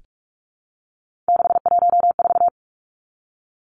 Morse code, heard as a keyed tone, about 44 wpm, 700 Hz. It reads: 614
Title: Numbers - 3-Digit 45wpm